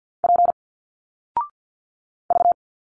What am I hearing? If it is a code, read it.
Morse code, heard as a keyed tone, about 50 wpm, 700 Hz. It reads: P V
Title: Single Letters 50wpm